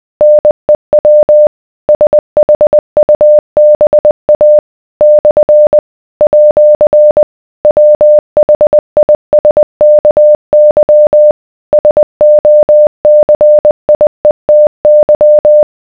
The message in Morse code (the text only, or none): NEW SHUBA / @ WHISKY SOCIETY